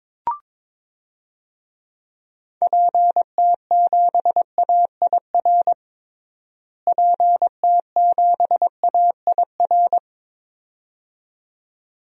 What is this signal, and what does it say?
Morse code, heard as a keyed tone, about 22 wpm, 700 Hz. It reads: PT7AIR PT7AIR
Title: DXCC-100 Call Signs - 6 Characters - Repeated 2x - 22wpm